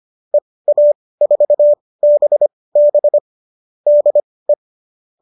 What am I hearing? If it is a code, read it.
Morse code, heard as a keyed tone, about 25 wpm, spaced as for 18 wpm, 570 Hz. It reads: EA4BB DE